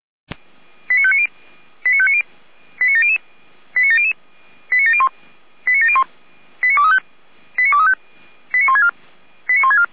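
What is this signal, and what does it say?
Five-tone selective call (ZVEI), heard as quick runs of five tones, tones of about 70 ms. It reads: 89590 89590 89700 89700 89701 89701 89235 89235 89164 89164
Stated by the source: ZVEI five-tone call 89120 received by SDR.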